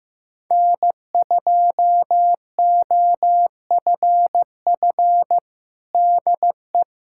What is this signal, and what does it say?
Morse code, heard as a keyed tone, about 15 wpm, 700 Hz. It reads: N2OFF DE
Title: USA Callsign DE Callsign - 5 Characters - 15wpm